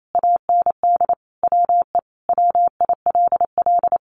Morse code, heard as a keyed tone, about 28 wpm, 700 Hz. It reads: AND WE WILL
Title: Sentences from Top 1400 Words 28wpm